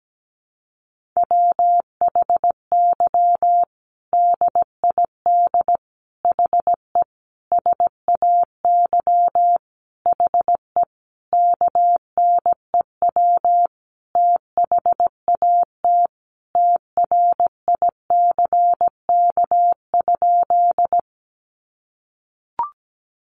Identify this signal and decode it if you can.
Morse code, heard as a keyed tone, about 17 wpm, 700 Hz. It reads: WHY DID HE SAY HE KNEW THAT TRICK?
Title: Sentences from Top 1500 Words 17wpm